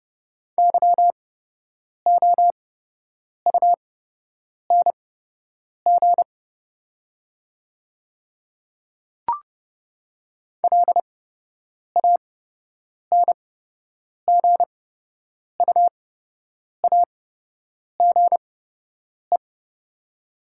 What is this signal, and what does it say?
Morse code, heard as a keyed tone, about 30 wpm, spaced as for 8 wpm, 700 Hz. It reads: YOUNG LANGUAGE
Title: 8x Character Spacing - Top 500 Words - 30wpm